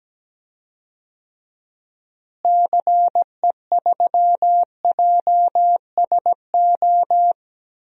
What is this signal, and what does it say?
Morse code, heard as a keyed tone, about 17 wpm, 700 Hz. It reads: CE3JSO
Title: DXCC-100 Call Signs - 6 Characters - Repeated 2x - 17wpm